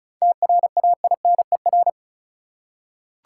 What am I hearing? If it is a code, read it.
Morse code, heard as a keyed tone, about 36 wpm, 700 Hz. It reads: TRAINER
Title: Seven Letter Words - 35wpm